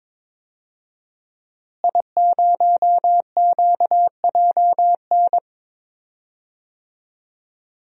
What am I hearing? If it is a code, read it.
Morse code, heard as a keyed tone, about 22 wpm, 700 Hz. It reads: I0QJN
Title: DXCC-100 Call Signs - 5 Characters - Repeated 3x - 22wpm